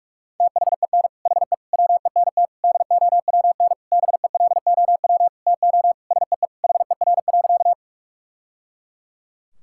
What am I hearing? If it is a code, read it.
Morse code, heard as a keyed tone, about 41 wpm, 690 Hz. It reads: THEN HE WENT DOWN BELOW TO SEE HER.
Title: Sentences from Top 1000 Words - Review Part 5 - 45wpm